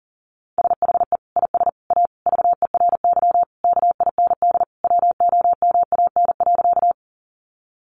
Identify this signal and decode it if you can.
Morse code, heard as a keyed tone, about 40 wpm, 700 Hz. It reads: SHE IS A VERY KIND WOMAN.
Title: Sentences from Top 1100 Words 40wpm